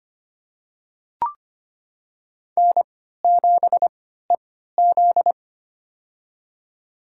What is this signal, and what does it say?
Morse code, heard as a keyed tone, about 25 wpm, spaced as for 14 wpm, 700 Hz. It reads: N7EZ
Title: USA Call Signs - 4 Characters - 3x Character Spacing - 25wpm